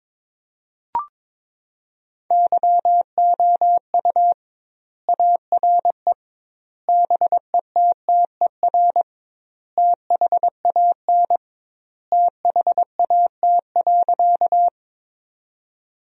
Morse code, heard as a keyed tone, about 22 wpm, 700 Hz. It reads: YOU ARE BETTER THAN THAT.
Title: 2x Word-Spacing - Sentences from Top 400 Words - 22wpm